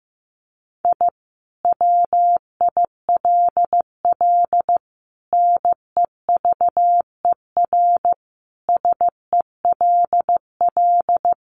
Morse code, heard as a keyed tone, about 15 wpm, 700 Hz. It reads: I WILL NEVER SELL
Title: Sentences from Top 1100 Words 15wpm